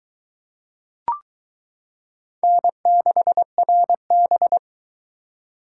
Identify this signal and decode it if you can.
Morse code, heard as a keyed tone, about 22 wpm, 700 Hz. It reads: N6RB